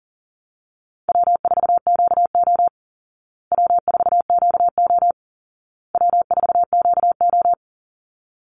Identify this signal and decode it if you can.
Morse code, heard as a keyed tone, about 40 wpm, 700 Hz. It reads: W4QO W4QO W4QO
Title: Easy Call Signs Repeated 3x - 40wpm